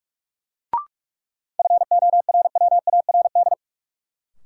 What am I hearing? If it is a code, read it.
Morse code, heard as a keyed tone, about 45 wpm, 700 Hz. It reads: FORWARD
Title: Sentences from Top 1300 Words - New Word Drill Alt - 45wpm